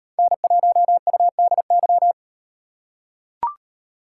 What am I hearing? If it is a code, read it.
Morse code, heard as a keyed tone, about 38 wpm, 700 Hz. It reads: N1UDY